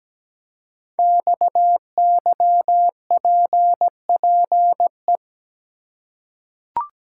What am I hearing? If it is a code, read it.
Morse code, heard as a keyed tone, about 17 wpm, 700 Hz. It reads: XYPPE
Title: Code Groups - 5 Letters - 17wpm